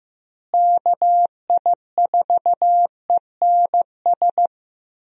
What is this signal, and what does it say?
Morse code, heard as a keyed tone, about 15 wpm, 700 Hz. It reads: KI4ENS
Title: USA Call Signs - 6 Characters - Repeated 3x - 15wpm